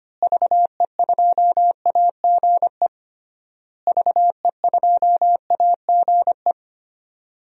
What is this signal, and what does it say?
Morse code, heard as a keyed tone, about 25 wpm, 700 Hz. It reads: VE2AGE VE2AGE
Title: DXCC-100 Call Signs - 6 Characters - Repeated 3x - 25wpm